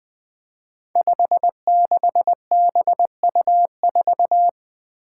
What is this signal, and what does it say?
Morse code, heard as a keyed tone, about 21 wpm, 700 Hz. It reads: 56BU4
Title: Five Challenging Letter-Number Combinations 20wpm